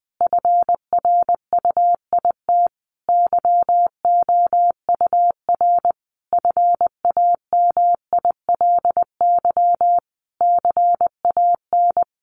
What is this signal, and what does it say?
Morse code, heard as a keyed tone, about 19 wpm, 700 Hz. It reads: FRUIT YOUR FAMILY CAN
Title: Sentences from Top 1300 Words 20wpm